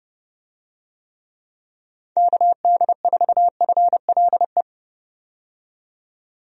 Morse code, heard as a keyed tone, about 32 wpm, 700 Hz. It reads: KD4FLE
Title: USA Call Signs - 6 Characters - 30wpm